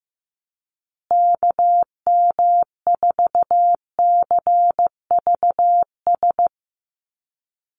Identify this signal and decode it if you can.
Morse code, heard as a keyed tone, about 15 wpm, 700 Hz. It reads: KM4CVS
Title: USA Call Signs Repeated 3x 15wpm